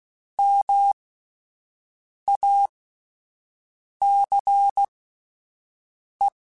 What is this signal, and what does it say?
Morse code, heard as a keyed tone, about 16 wpm, spaced as for 5 wpm, 800 Hz. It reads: MACE
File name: Alphabetical / Alphabetical mp3s / Ms @5